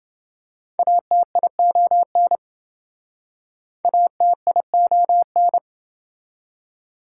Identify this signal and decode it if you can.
Morse code, heard as a keyed tone, about 26 wpm, 700 Hz. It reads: ATION ATION